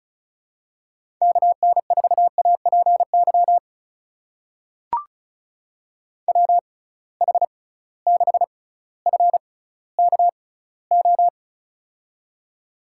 Morse code, heard as a keyed tone, about 36 wpm, spaced as for 18 wpm, 700 Hz. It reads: KN4APY WH6FKO
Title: USA Call Signs - 6 Characters - 6x Character Spacing - 35wpm